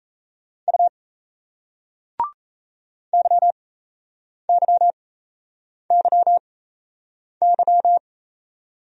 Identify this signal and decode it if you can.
Morse code, heard as a keyed tone, about 36 wpm, spaced as for 8 wpm, 700 Hz. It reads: U YYYY